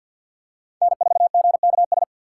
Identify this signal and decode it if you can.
Morse code, heard as a keyed tone, about 51 wpm, 700 Hz. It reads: N4GKS
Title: USA Call Signs - 5 Characters - 4x Character Spacing - 50wpm